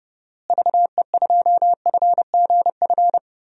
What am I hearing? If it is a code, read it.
Morse code, heard as a keyed tone, about 30 wpm, 700 Hz. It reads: VE2FGF
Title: DXCC-100 Call Signs - 6 Characters - Repeated 2x - 30wpm